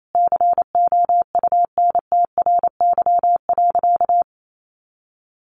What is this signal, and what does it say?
Morse code, heard as a keyed tone, about 28 wpm, 700 Hz. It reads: COUNTRY.
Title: Sentences from Top 300 Words 28wpm